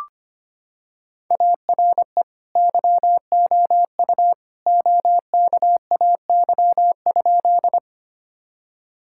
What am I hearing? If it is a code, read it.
Morse code, heard as a keyed tone, about 25 wpm, 700 Hz. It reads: ARE YOU OKAY?
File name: practice / 25wpm / Sentences with 2-Word Phrases - 25wpm